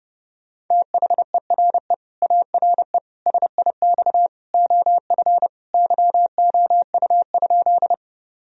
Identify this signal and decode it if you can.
Morse code, heard as a keyed tone, about 30 wpm, 700 Hz. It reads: THERE ARE SIX OF YOU?
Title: Sentences from Top 1000 Words - Review Part 2 - 30wpm